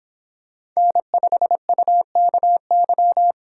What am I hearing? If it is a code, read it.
Morse code, heard as a keyed tone, about 25 wpm, 700 Hz. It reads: N5UKY